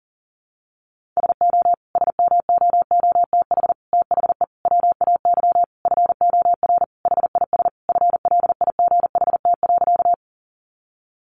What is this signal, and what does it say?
Morse code, heard as a keyed tone, about 40 wpm, 700 Hz. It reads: SO SMOOTH THE WAY FOR HIS FLIGHT.
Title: Sentences from Top 1400 Words 40wpm